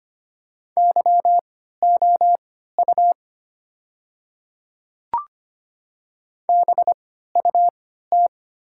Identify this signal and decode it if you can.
Morse code, heard as a keyed tone, about 25 wpm, spaced as for 14 wpm, 700 Hz. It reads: YOU BUT
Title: 3x Character Spacing - Top 100 Words - 25wpm